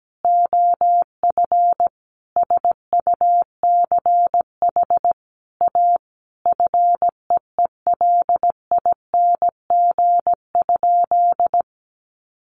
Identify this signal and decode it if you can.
Morse code, heard as a keyed tone, about 17 wpm, 700 Hz. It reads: OF SUCH A FEELING?
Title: Sentences from Top 1200 Words 17wpm